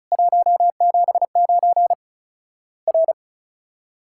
Morse code, heard as a keyed tone, about 34 wpm, 700 Hz. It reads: 179 R
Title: Worked All Europe DX Contest - QTC Exchange - 35wpm